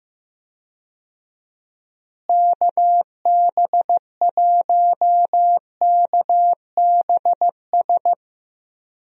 Morse code, heard as a keyed tone, about 15 wpm, 700 Hz. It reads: KB1KBS
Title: USA Call Signs - 6 Characters - Repeated 3x - 15wpm